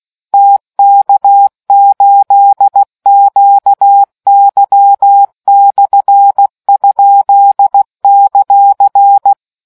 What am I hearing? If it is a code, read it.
Morse code, heard as a keyed tone, about 16 wpm, 800 Hz. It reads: TK8QY/?;